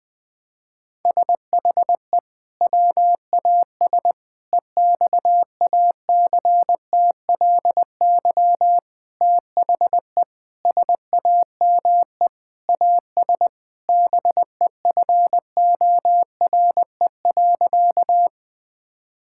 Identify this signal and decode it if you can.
Morse code, heard as a keyed tone, about 20 wpm, 700 Hz. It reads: SHE WAS EXACTLY THE SAME AS BEFORE.
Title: Sentences from Top 1000 Words - Review Part 6 - 20wpm